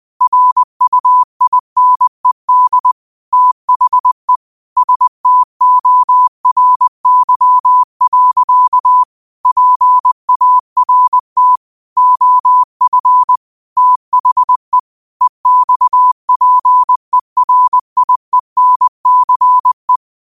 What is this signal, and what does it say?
Morse code, heard as a keyed tone, about 20 wpm, 1000 Hz. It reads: RUINED THE STORY. PART OF THE EXPERIENCE
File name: qotdfiles / 20wpm